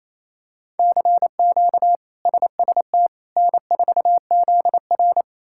Morse code, heard as a keyed tone, about 28 wpm, 700 Hz. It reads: CQ SST N4ZR
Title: K1USN SST QSO Practice - 28wpm